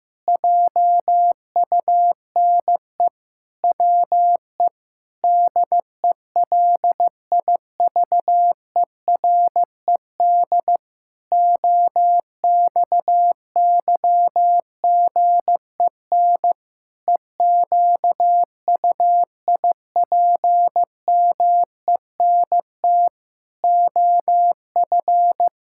Morse code, heard as a keyed tone, about 16 wpm, 700 Hz. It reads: JUNE WE DELIVERED OXYGEN EQUIPMENT OF
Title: Pangram Challenge v2 - 15wpm